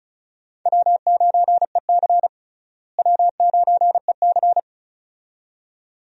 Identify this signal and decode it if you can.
Morse code, heard as a keyed tone, about 36 wpm, 700 Hz. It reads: W9EC W9EC
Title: USA Call Signs - 4 Characters - Repeated 2x - 35wpm